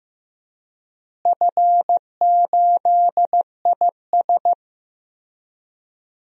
Morse code, heard as a keyed tone, about 15 wpm, 700 Hz. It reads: F8IS
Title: DXCC-100 Call Signs - 4 Characters - Repeated 3x - 15wpm